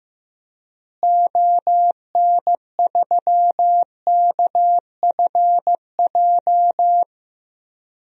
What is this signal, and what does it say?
Morse code, heard as a keyed tone, about 15 wpm, 700 Hz. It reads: ON3KFJ
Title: DXCC-100 Call Signs - 6 Characters - Repeated 3x - 15wpm